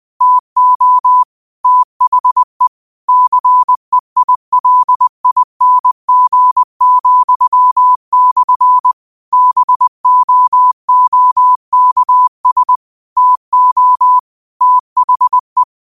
Morse code, heard as a keyed tone, about 20 wpm, 1000 Hz. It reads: TO THE CEILING,/ BOOKS TO THE
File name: qotdfiles / 20wpm